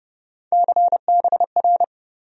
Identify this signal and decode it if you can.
Morse code, heard as a keyed tone, about 30 wpm, 700 Hz. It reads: CBR
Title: USA Call Sign Postfixes - 3 Letters 30wpm